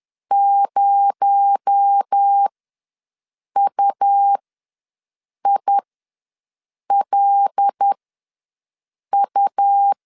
Morse code, heard as a keyed tone, about 10 wpm, spaced as for 6 wpm, 800 Hz. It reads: 0UILU
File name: LEZIONI / 6 PKX0 / 6 PKX0 ws 10wpm 4220